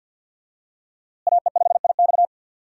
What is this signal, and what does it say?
Morse code, heard as a keyed tone, about 52 wpm, 700 Hz. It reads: AE5IX
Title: USA Call Signs - 5 Characters - Repeated 3x - 50wpm